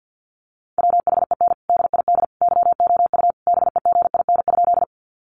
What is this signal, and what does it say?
Morse code, heard as a keyed tone, about 50 wpm, 700 Hz. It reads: WHEN DID YOU BEGIN?
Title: Sentences from Top 300 Words 50wpm